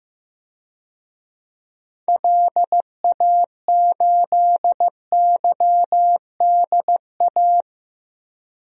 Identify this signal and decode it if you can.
Morse code, heard as a keyed tone, about 15 wpm, 700 Hz. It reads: LA8YDA
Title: DXCC-100 Call Signs - 6 Characters - 15wpm